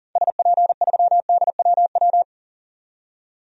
Morse code, heard as a keyed tone, about 40 wpm, 700 Hz. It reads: SP3DWW
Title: DXCC-100 Call Signs - 6 Characters - 40wpm